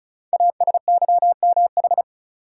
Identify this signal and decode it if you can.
Morse code, heard as a keyed tone, about 36 wpm, 690 Hz. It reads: ASYMH
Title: Code Groups - 5 Letters - 35wpm